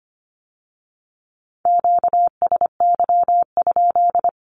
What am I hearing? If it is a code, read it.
Morse code, heard as a keyed tone, about 25 wpm, 700 Hz. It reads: QSY?
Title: Q-Codes with Spoken Meaning - 25wpm